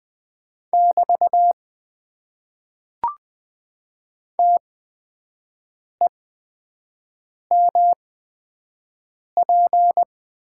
Morse code, heard as a keyed tone, about 20 wpm, spaced as for 5 wpm, 700 Hz. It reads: <BT> TEMP
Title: Top 90 QSO Elements - 8x Character Spacing - 20wpm